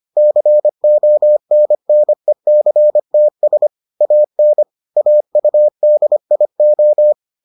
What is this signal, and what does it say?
Morse code, heard as a keyed tone, about 25 wpm, 600 Hz. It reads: CONNECTS AN AUDIO